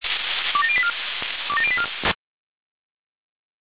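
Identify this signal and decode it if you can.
Five-tone selective call (ZVEI), heard as quick runs of five tones, tones of about 70 ms. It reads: 27084 27084
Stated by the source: ZVEI five-tone selective call 27002 received on SDR.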